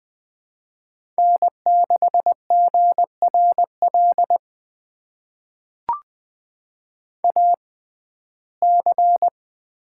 Morse code, heard as a keyed tone, about 20 wpm, spaced as for 15 wpm, 700 Hz. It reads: N6GRL AC
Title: USA Call Signs - 5 Characters - 6x Character Spacing - 20wpm